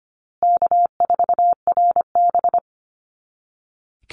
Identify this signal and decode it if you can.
Morse code, heard as a keyed tone, about 25 wpm, 700 Hz. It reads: K4RB
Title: Easy Call Signs Repeated 2x - 25wpm